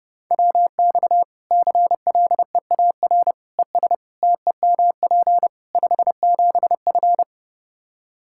Morse code, heard as a keyed tone, about 30 wpm, 700 Hz. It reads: WX CLEAR ES TEMP 57F
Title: Practice QSO - Exchange 2 - Info - 30wpm